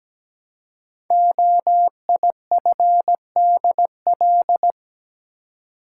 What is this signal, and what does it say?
Morse code, heard as a keyed tone, about 17 wpm, 700 Hz. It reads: OIFDL
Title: Code Groups - 5 Letters - 17wpm